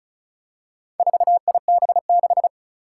Morse code, heard as a keyed tone, about 36 wpm, 700 Hz. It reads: 4IB6
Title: Four Challenging Letter-Number Combinations 35wpm